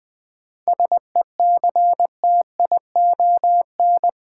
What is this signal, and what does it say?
Morse code, heard as a keyed tone, about 20 wpm, 700 Hz. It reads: SECTION
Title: Sentences from Top 1000 Words - Review Part 5 - New Word Drill - 20wpm